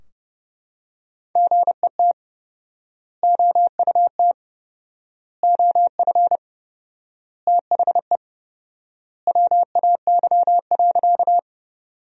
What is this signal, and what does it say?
Morse code, heard as a keyed tone, about 30 wpm, 700 Hz. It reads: GET OUT OF THE WAY.
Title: 4x Word-Spacing - Sentences from Top 200 Words - 30wpm